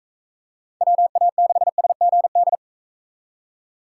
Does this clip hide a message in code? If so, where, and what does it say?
Yes: Morse code from 0.8 to 2.6 s, heard as a keyed tone, about 40 wpm, 700 Hz. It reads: WA6SGD